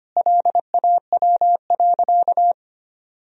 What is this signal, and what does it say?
Morse code, heard as a keyed tone, about 25 wpm, 700 Hz. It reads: LAW.
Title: Sentences from Top 1000 Words - Review Part 3 - 25wpm